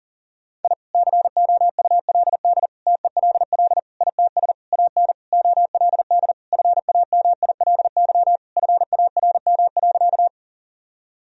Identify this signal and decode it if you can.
Morse code, heard as a keyed tone, about 37 wpm, 700 Hz. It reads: I COULD TELL ITS AN OLD FAMILY FARM.
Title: Sentences from Top 1000 Words - Review Part 2 - 40wpm